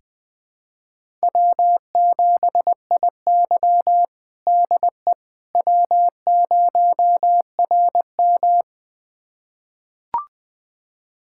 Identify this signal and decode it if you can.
Morse code, heard as a keyed tone, about 20 wpm, 700 Hz. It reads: W7IY DE W0RM
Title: USA Callsign DE Callsign - 4 Characters - 20wpm